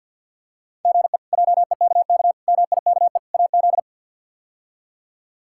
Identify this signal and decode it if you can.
Morse code, heard as a keyed tone, about 53 wpm, 700 Hz. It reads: GE JEKK MIKE AZ
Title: K1USN SST QSO Practice - 50wpm